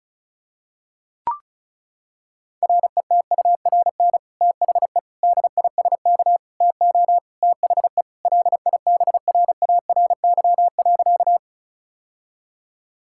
Morse code, heard as a keyed tone, about 36 wpm, 700 Hz. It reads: RETURN THE DISK TO THE LIBRARY.
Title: Sentences from Top 1000 Words - Review Part 6 - 35wpm